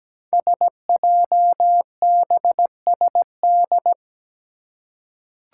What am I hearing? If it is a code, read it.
Morse code, heard as a keyed tone, about 17 wpm, 700 Hz. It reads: SJBSD
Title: Five Challenging Letter-Number Combinations 17wpm